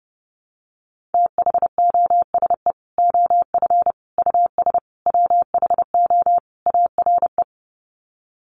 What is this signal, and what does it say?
Morse code, heard as a keyed tone, about 30 wpm, 700 Hz. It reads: THOSE OF US WHO ARE
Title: Sets of 5 Words Encore2 30wpm